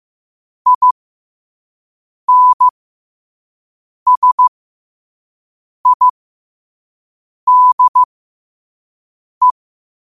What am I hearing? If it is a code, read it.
Morse code, heard as a keyed tone, about 15 wpm, spaced as for 5 wpm, 1000 Hz. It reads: INSIDE
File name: qotdfiles / 5wpm